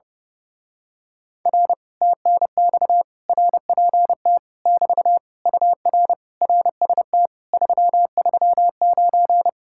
Morse code, heard as a keyed tone, about 30 wpm, 700 Hz. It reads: R TNX RPT <BT> UR RST 339
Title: Speed-Racing Common QSO Phrases - 30wpm